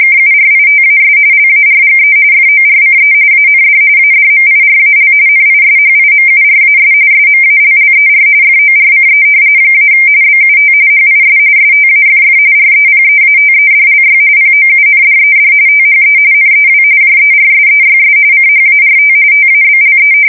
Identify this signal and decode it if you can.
RTTY, about 45.45 baud. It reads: KURITAS LARGE FORCE AND ADMIRAL NISHIMURAS SMALLER FORCE - BUT HIS STAFF REPEATEDLY POUNDED THE NEW JERSEYS CHART TABLE
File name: fsk170 / battleshipnj